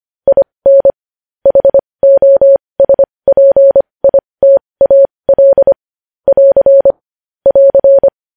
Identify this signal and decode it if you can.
Morse code, heard as a keyed tone, about 25 wpm, spaced as for 20 wpm, 550 Hz. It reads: IN HOSPITAL <AR> <AR>
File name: podcast / News headlines-1x-25x20